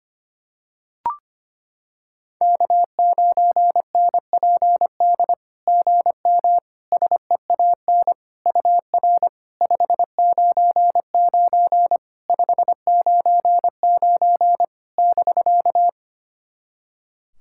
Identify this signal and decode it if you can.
Morse code, heard as a keyed tone, about 25 wpm, 690 Hz. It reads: K9NPD GM SEAN UR 599 599 <BK>
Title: SOTA QSO Practice 25wpm